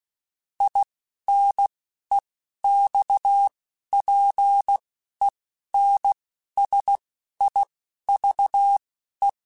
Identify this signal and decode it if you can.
Morse code, heard as a keyed tone, about 16 wpm, spaced as for 11 wpm, 800 Hz. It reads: INEXPENSIVE
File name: Quotes / Quotes-mp3s / A Smile @10